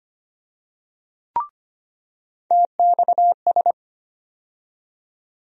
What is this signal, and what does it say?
Morse code, heard as a keyed tone, about 25 wpm, 700 Hz. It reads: TXS
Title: USA Call Sign Postfixes - 3 Letters 25wpm